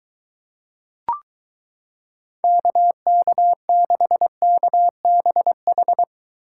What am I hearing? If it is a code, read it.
Morse code, heard as a keyed tone, about 23 wpm, 700 Hz. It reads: KK6KBH